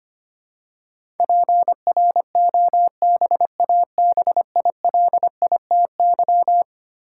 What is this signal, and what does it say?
Morse code, heard as a keyed tone, about 25 wpm, 700 Hz. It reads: PROBABILITY
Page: morse-code-ninja-practice-25wpm